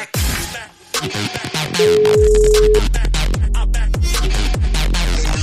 Ringback tone (Europe, 425 Hz), with noise.